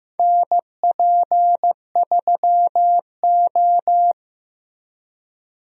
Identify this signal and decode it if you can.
Morse code, heard as a keyed tone, about 15 wpm, 700 Hz. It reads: NP3O
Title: USA Call Signs - 4 Characters - Repeated 2x - 15wpm